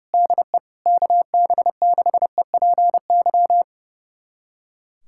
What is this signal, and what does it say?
Morse code, heard as a keyed tone, about 30 wpm, 700 Hz. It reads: DE KB6EPY
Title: USA Callsign DE Callsign 30wpm